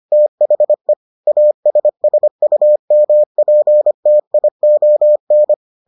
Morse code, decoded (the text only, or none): THE ASSUMPTION